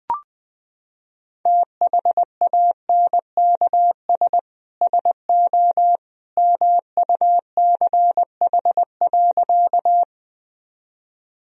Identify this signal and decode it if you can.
Morse code, heard as a keyed tone, about 20 wpm, 700 Hz. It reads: THANKS SO MUCH.
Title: Sentences from Top 1000 Words - Review Part 3 - 20wpm